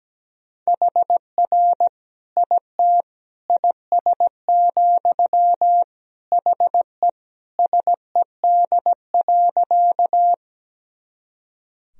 Morse code, heard as a keyed tone, about 17 wpm, 700 Hz. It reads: HR IT IS, HE SED.
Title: Sentences from Top 200 Words and Abbreviations 17wpm